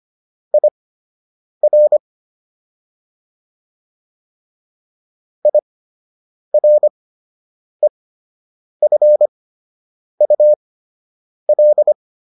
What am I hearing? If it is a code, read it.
Morse code, heard as a keyed tone, about 25 wpm, spaced as for 8 wpm, 600 Hz. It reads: IR IREFUL